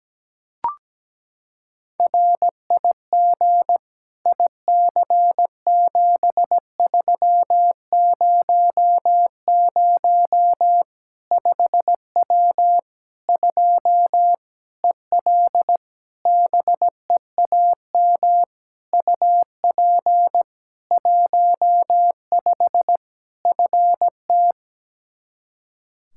Morse code, heard as a keyed tone, about 17 wpm, 700 Hz. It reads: RIG IC7300 5W 2 EL BEAM UP 15 FT
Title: K1USN SST QSO Practice 17wpm